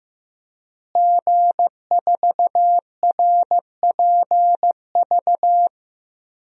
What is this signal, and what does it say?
Morse code, heard as a keyed tone, about 16 wpm, 700 Hz. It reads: G4RPV
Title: DXCC-100 Call Signs - 5 Characters - Repeated 3x - 15wpm